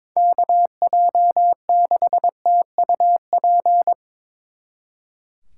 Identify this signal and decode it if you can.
Morse code, heard as a keyed tone, about 25 wpm, 700 Hz. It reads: KJ6TUP
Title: Speed Racing - USA 2x3 Call Signs 22wpm